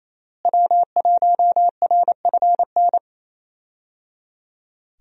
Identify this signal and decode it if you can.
Morse code, heard as a keyed tone, about 28 wpm, 700 Hz. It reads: W1RFN
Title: USA Call Signs - 5 Characters - Repeated 3x - 28wpm